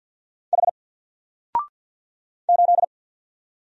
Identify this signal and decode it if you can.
Morse code, heard as a keyed tone, about 48 wpm, 700 Hz. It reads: H 8
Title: Single Letter-Number 50wpm